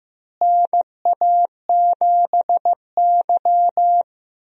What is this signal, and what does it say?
Morse code, heard as a keyed tone, about 15 wpm, 700 Hz. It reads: NA7Y
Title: USA Call Signs - 4 Characters - Repeated 3x - 15wpm